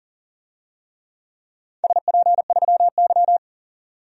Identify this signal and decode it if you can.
Morse code, heard as a keyed tone, about 40 wpm, 700 Hz. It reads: SP3Y
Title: DXCC-100 Call Signs - 4 Characters - 40wpm